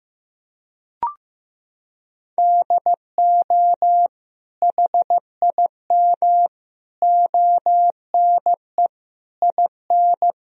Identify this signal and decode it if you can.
Morse code, heard as a keyed tone, about 15 wpm, 700 Hz. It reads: DO HIM ONE IN
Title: Sentences from Top 1000 Words - Review Part 1 - 15wpm